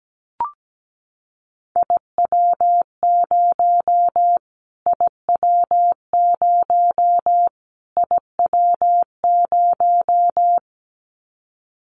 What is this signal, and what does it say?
Morse code, heard as a keyed tone, about 17 wpm, 700 Hz. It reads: IW0 IW0 IW0
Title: Callsign Prefixes for Top 25 DXCC - Repeated 3x - 17wpm